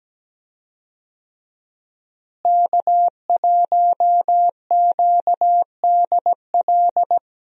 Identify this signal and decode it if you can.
Morse code, heard as a keyed tone, about 17 wpm, 700 Hz. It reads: K1QDL
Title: USA Call Signs - 5 Characters - Repeated 3x - 17wpm